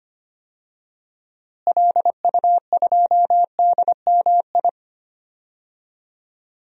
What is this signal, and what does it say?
Morse code, heard as a keyed tone, about 25 wpm, 700 Hz. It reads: LU2DMI
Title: DXCC-100 Call Signs - 6 Characters - Repeated 3x - 25wpm